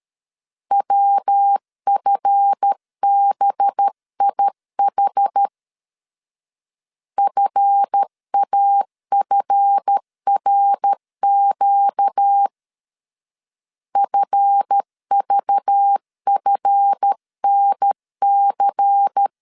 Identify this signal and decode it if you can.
Morse code, heard as a keyed tone, about 13 wpm, 800 Hz. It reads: WFBIH FAFRQ FVFNC
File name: LEZIONI / 7 GF9Z / 7 GF 12wpm 2122